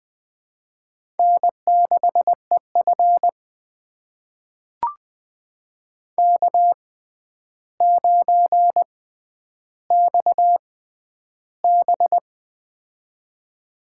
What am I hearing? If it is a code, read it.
Morse code, heard as a keyed tone, about 20 wpm, spaced as for 10 wpm, 700 Hz. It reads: N6EF K9XB
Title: USA Call Signs - 4 Characters - 6x Character Spacing - 20wpm